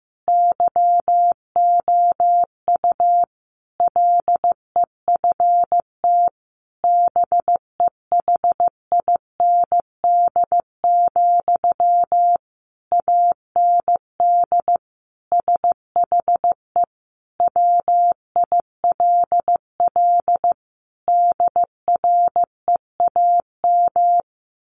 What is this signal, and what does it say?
Morse code, heard as a keyed tone, about 15 wpm, 700 Hz. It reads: YOU LEFT BEHIND, AND SHE WILL DREAM
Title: Sentences from Top 1500 Words 15wpm